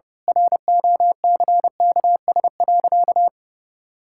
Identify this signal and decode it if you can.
Morse code, heard as a keyed tone, about 32 wpm, 700 Hz. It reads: ROCKS.
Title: Pangram Challenge v2 - 30wpm